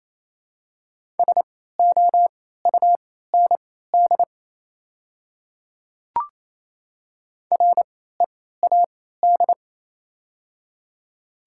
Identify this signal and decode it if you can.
Morse code, heard as a keyed tone, about 28 wpm, spaced as for 16 wpm, 700 Hz. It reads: SOUND READ
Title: 3x Character Spacing - Top 200 Words - 28wpm